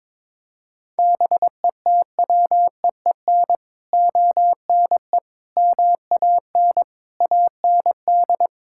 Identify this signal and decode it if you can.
Morse code, heard as a keyed tone, about 22 wpm, 700 Hz. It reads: BETWEEN ONE MAN AND
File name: practice / 22wpm / Sets of 6 Words - 22wpm